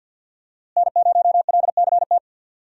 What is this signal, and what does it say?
Morse code, heard as a keyed tone, about 49 wpm, 700 Hz. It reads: N0LCT
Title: USA Call Signs - 5 Characters - Repeated 2x - 50wpm